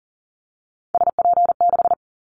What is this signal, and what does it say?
Morse code, heard as a keyed tone, about 40 wpm, 700 Hz. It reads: SP6
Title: Callsign Prefixes for Top 25 DXCC - 40wpm